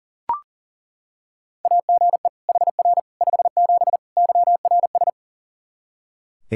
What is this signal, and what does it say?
Morse code, heard as a keyed tone, about 37 wpm, 700 Hz. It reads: AGE HR 57 YRS
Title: Practice QSO - Exchange 3 - More Info - 40wpm